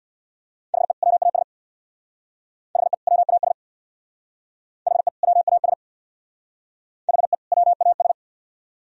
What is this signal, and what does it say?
Morse code, heard as a keyed tone, about 60 wpm, 700 Hz. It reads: HE WAS HE WAS HE WAS HE WAS
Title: Speed-Racing - Sets of 2 Words 50wpm